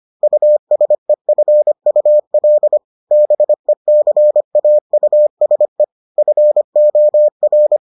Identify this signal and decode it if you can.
Morse code, heard as a keyed tone, about 25 wpm, 600 Hz. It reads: USEFUL BECAUSE FOR